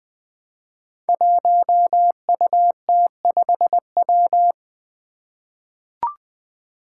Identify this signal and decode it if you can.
Morse code, heard as a keyed tone, about 20 wpm, 700 Hz. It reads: 1UT5W
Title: Five Challenging Letter-Number Combinations 20wpm